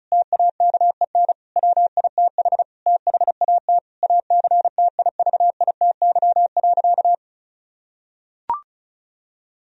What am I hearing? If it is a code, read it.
Morse code, heard as a keyed tone, about 34 wpm, 700 Hz. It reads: TAKEN WITH THAT ACTIVITY.
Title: Sentences from Top 1000 Words - Review Part 4 - 35wpm